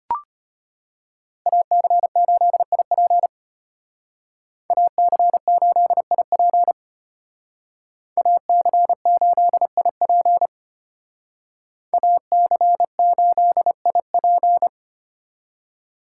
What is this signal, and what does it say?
Morse code, heard as a keyed tone, about 32 wpm, 700 Hz. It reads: AC8IP AC8IP AC8IP AC8IP